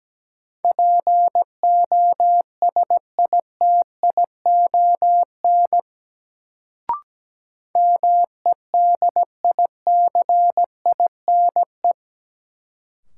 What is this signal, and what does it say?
Morse code, heard as a keyed tone, about 17 wpm, 700 Hz. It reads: POSITION MEDICINE
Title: Sentences from Top 900 Words - New Word Drill - 17wpm